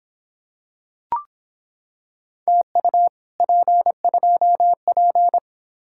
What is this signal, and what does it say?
Morse code, heard as a keyed tone, about 26 wpm, 700 Hz. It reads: TU P2P